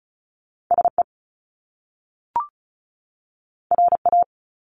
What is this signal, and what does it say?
Morse code, heard as a keyed tone, about 36 wpm, 700 Hz. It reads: SE RA
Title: Top Two Letter Combinations in Words 35wpm